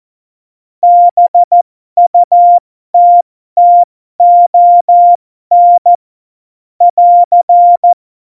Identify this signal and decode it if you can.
Morse code, heard as a keyed tone, about 14 wpm, spaced as for 12 wpm, 700 Hz. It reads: BUTTON <AR>
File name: Morsekurs / Witze